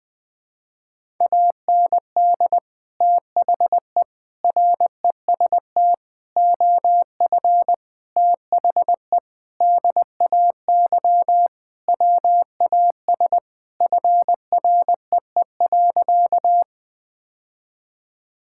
Morse code, heard as a keyed tone, about 20 wpm, 700 Hz. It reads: AND THE REST OF THE DAY WAS FREE.
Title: Sentences with 4-Word Phrases - 20wpm